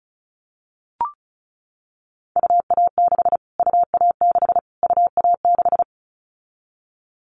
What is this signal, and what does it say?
Morse code, heard as a keyed tone, about 34 wpm, 700 Hz. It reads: UA6 UA6 UA6
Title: Callsign Prefixes for Top 25 DXCC - Repeated 3x - 35wpm